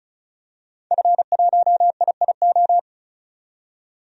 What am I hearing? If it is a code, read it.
Morse code, heard as a keyed tone, about 36 wpm, 700 Hz. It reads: F1IIO
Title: DXCC-100 Call Signs - 5 Characters - 35wpm